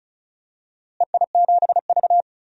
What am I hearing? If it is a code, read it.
Morse code, heard as a keyed tone, about 36 wpm, 690 Hz. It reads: EI7V
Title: DXCC-100 Call Signs - 4 Characters - Repeated 2x - 35wpm